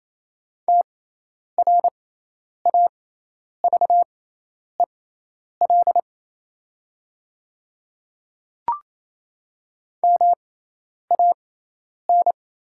Morse code, heard as a keyed tone, about 28 wpm, spaced as for 10 wpm, 700 Hz. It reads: TRAVEL MAN
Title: 6x Character Spacing - Top 500 Words - 28wpm